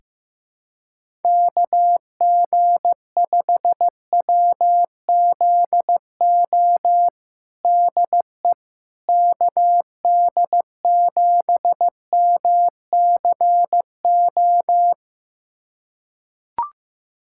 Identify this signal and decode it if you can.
Morse code, heard as a keyed tone, about 15 wpm, 700 Hz. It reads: KG5WZO DE KD7MCO
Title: USA Callsign DE Callsign 15wpm